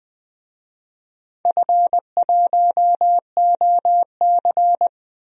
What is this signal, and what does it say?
Morse code, heard as a keyed tone, about 20 wpm, 700 Hz. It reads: F1OC
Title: DXCC-100 Call Signs - 4 Characters - 20wpm